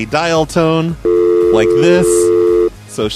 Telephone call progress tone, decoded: dial tone